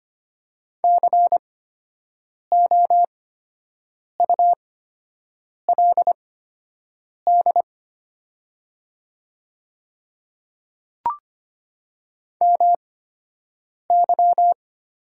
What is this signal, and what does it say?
Morse code, heard as a keyed tone, about 25 wpm, spaced as for 7 wpm, 700 Hz. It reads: COULD MY
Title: 8x Character Spacing - Top 500 Words - 25wpm